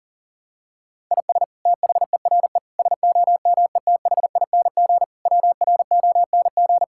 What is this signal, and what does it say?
Morse code, heard as a keyed tone, about 40 wpm, 700 Hz. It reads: IS THERE SOMETHING WRONG
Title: Sentences from Top 1000 Words - Review Part 6 - 40wpm